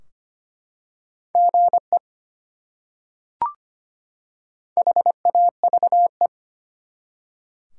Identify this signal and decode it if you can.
Morse code, heard as a keyed tone, about 25 wpm, 700 Hz. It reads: GE HAVE
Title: Top 180 QSO Elements 25wpm